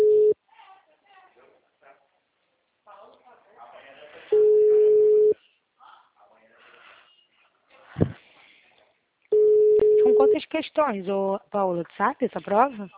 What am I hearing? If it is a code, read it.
Ringback tone (Europe, 425 Hz), with speech.